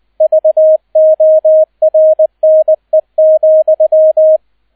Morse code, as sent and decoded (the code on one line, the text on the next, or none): ...- --- .-. -. --..--
VORNE,